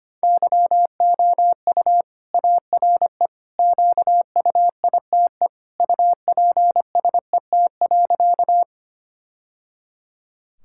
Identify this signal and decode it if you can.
Morse code, heard as a keyed tone, about 25 wpm, 700 Hz. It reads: YOU ARE QUITE UPSET.
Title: Sentences from Top 1000 Words - Review Part 5 - 25wpm